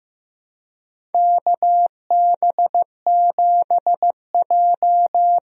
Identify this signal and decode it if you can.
Morse code, heard as a keyed tone, about 16 wpm, 700 Hz. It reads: KB7J
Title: USA Call Signs - 4 Characters - 15wpm